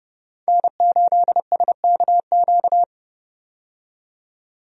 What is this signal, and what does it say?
Morse code, heard as a keyed tone, about 30 wpm, 700 Hz. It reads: N8SKQ
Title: USA Call Signs - 5 Characters - 30wpm